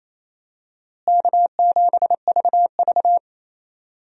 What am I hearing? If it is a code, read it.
Morse code, heard as a keyed tone, about 28 wpm, 700 Hz. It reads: K7VV
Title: USA Call Signs - 4 Characters - Repeated 3x - 28wpm